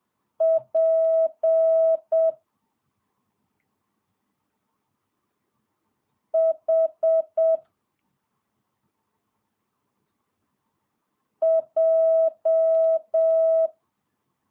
Morse code, heard as a keyed tone, about 7 wpm, 650 Hz. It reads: P H J
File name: jh8cbh / 2022cw